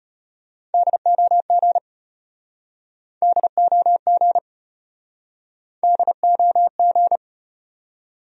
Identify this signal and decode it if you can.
Morse code, heard as a keyed tone, about 35 wpm, 700 Hz. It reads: DOG DOG DOG